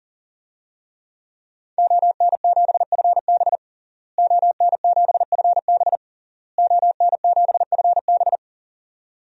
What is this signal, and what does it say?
Morse code, heard as a keyed tone, about 37 wpm, 700 Hz. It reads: ON7FB ON7FB ON7FB
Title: DXCC-100 Call Signs - 5 Characters - Repeated 3x - 40wpm